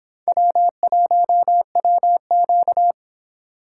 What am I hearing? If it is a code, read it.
Morse code, heard as a keyed tone, about 24 wpm, 700 Hz. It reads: W1WQ